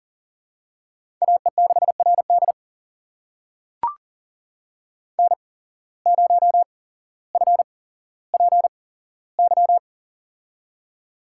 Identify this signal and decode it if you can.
Morse code, heard as a keyed tone, about 40 wpm, spaced as for 17 wpm, 700 Hz. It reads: AE6RD N0FPY
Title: USA Call Signs - 5 Characters - 8x Character Spacing - 40wpm